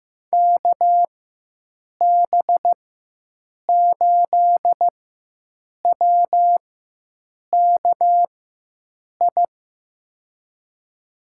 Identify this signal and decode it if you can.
Morse code, heard as a keyed tone, about 15 wpm, spaced as for 7 wpm, 700 Hz. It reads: KB8WKI
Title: USA Call Signs - 6 Characters - 4x Character Spacing - 15wpm